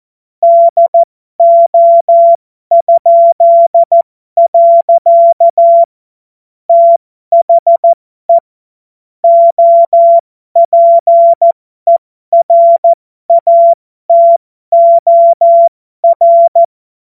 Morse code, heard as a keyed tone, about 14 wpm, spaced as for 12 wpm, 680 Hz. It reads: DO?. THE OPERATOR
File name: Morsekurs / Witze